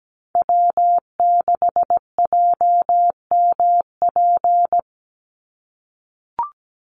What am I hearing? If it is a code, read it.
Morse code, heard as a keyed tone, about 17 wpm, 700 Hz. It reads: W6JMP
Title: Call Signs - 17wpm